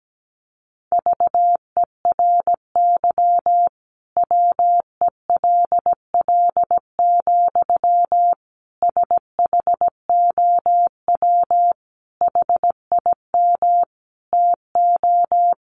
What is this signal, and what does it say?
Morse code, heard as a keyed tone, about 17 wpm, 700 Hz. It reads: VERY WELL, SHOW HIM TO
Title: Sentences from Top 200 Words 17wpm